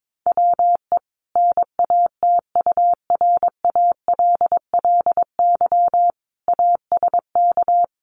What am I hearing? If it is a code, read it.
Morse code, heard as a keyed tone, about 22 wpm, 700 Hz. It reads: WE NATURALLY ASK
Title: Sentences from Top 1300 Words 22wpm